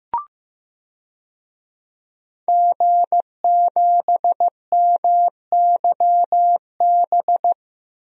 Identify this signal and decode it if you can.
Morse code, heard as a keyed tone, about 15 wpm, 700 Hz. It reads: G7MYB
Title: DXCC-100 Call Signs - 5 Characters - 15wpm